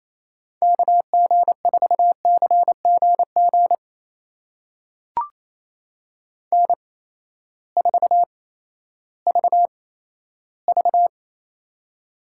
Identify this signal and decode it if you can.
Morse code, heard as a keyed tone, about 28 wpm, spaced as for 14 wpm, 700 Hz. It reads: KG4CGG N4VV
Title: USA Call Signs - 8x Character Spacing - 28wpm